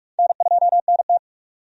Morse code, heard as a keyed tone, about 46 wpm, 700 Hz. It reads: N2NT
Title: USA Call Signs - 4 Characters - 45wpm